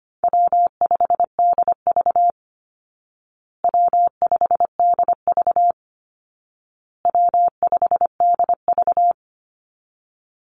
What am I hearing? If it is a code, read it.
Morse code, heard as a keyed tone, about 25 wpm, 700 Hz. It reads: W5DV W5DV W5DV
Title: Easy Call Signs Repeated 3x - 25wpm